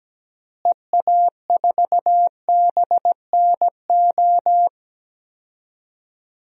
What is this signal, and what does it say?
Morse code, heard as a keyed tone, about 17 wpm, 700 Hz. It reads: EA4BNO
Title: DXCC-100 Call Signs - 6 Characters - Repeated 2x - 17wpm